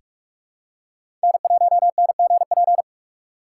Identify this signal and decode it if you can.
Morse code, heard as a keyed tone, about 44 wpm, 700 Hz. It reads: N1NGP
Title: USA Call Signs - 4x Character Spacing - 45wpm